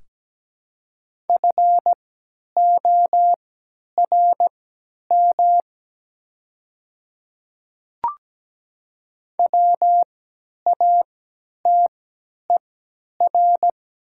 Morse code, heard as a keyed tone, about 17 wpm, spaced as for 10 wpm, 700 Hz. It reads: FORM WATER
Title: 3x Character Spacing - Top 500 Words - 17wpm